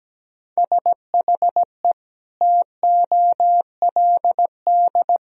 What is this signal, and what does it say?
Morse code, heard as a keyed tone, about 17 wpm, 700 Hz. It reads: SHE TOLD
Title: Sentences from Top 1000 Words - Review Part 4 - 17wpm